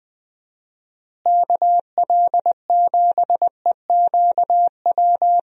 Morse code, heard as a keyed tone, about 20 wpm, 700 Hz. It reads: KL7EQW
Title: DXCC-100 Call Signs - 6 Characters - Repeated 3x - 20wpm